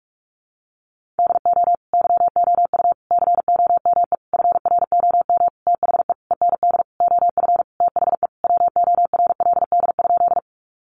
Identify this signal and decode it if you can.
Morse code, heard as a keyed tone, about 44 wpm, 700 Hz. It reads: DO YOU COME FROM THE END OF THE WORLD?
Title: Sentences from Top 200 Words 45wpm